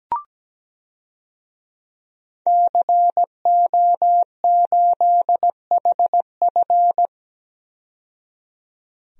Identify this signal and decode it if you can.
Morse code, heard as a keyed tone, about 17 wpm, 700 Hz. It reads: CO8HF
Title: DXCC-100 Call Signs - 5 Characters - 17wpm